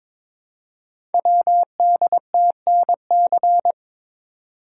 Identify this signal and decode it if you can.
Morse code, heard as a keyed tone, about 22 wpm, 700 Hz. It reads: WDTNC
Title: Code Groups - 5 Letters - 22wpm